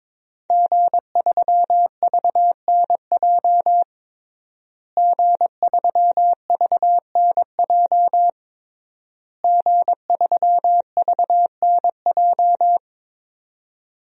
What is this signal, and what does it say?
Morse code, heard as a keyed tone, about 22 wpm, 700 Hz. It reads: G3VNJ G3VNJ G3VNJ
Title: DXCC-100 Call Signs - 5 Characters - Repeated 3x - 22wpm